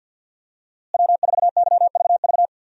Morse code, heard as a keyed tone, about 52 wpm, 700 Hz. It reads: W4YVV
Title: USA Call Signs - 5 Characters - Repeated 2x - 50wpm